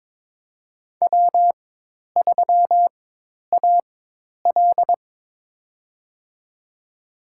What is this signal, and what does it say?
Morse code, heard as a keyed tone, about 22 wpm, spaced as for 10 wpm, 700 Hz. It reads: W3AL
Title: USA Call Signs - 4 Characters - 4x Character Spacing - 22wpm